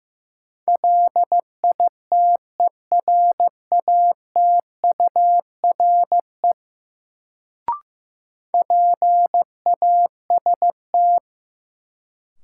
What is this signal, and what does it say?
Morse code, heard as a keyed tone, about 15 wpm, 700 Hz. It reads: LITERATURE PAST
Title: Sentences from Top 600 Words - New Word Drill - 15wpm